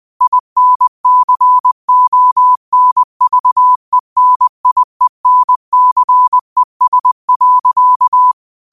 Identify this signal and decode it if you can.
Morse code, heard as a keyed tone, about 20 wpm, 1000 Hz. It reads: INCONVENIENCES.
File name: qotdfiles / 20wpm